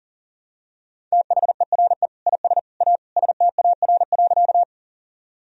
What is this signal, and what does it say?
Morse code, heard as a keyed tone, about 40 wpm, 700 Hz. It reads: THERE IS A STAR.
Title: Sentences from Top 1000 Words - Review Part 2 - 40wpm